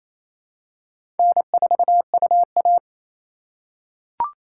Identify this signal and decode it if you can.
Morse code, heard as a keyed tone, about 27 wpm, 700 Hz. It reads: N4UA
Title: USA Call Signs - 4 Characters - 4x Character Spacing - 28wpm